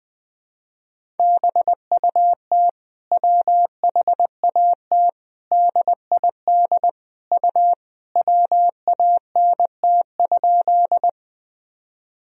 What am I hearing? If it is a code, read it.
Morse code, heard as a keyed tone, about 20 wpm, 700 Hz. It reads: BUT WHAT DID U WANT?
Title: Sentences from Top 200 Words and Abbreviations 20wpm